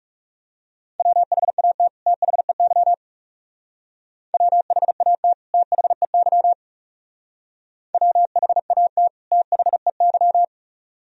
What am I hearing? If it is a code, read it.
Morse code, heard as a keyed tone, about 36 wpm, 700 Hz. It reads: WHAT THEY WHAT THEY WHAT THEY
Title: Speed-Racing - Sets of 2 Words 30wpm